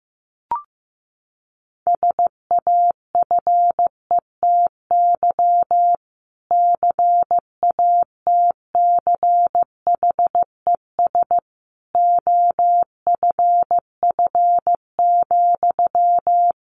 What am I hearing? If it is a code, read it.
Morse code, heard as a keyed tone, about 15 wpm, 700 Hz. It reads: SAFETY CATCHES OFF,
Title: A Sound Of Thunder 15wpm